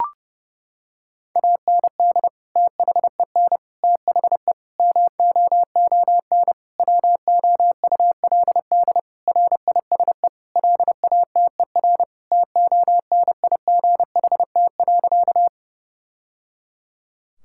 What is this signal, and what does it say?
Morse code, heard as a keyed tone, about 30 wpm, 700 Hz. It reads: AND THEN THE MOON WOULD RISE LATER TONIGHT.
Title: Sentences from Top 1000 Words - Review Part 6 - 30wpm